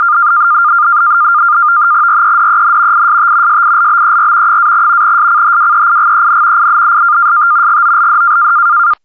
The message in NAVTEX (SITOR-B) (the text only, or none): REDEN IST SILBER - SCHWEIGEN IST GOLD